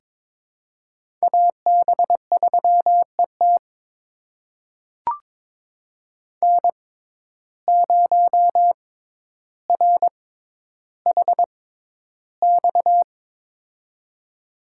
Morse code, heard as a keyed tone, about 22 wpm, spaced as for 11 wpm, 700 Hz. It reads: AB3ET N0RHX
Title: USA Call Signs - 5 Characters - 6x Character Spacing - 22wpm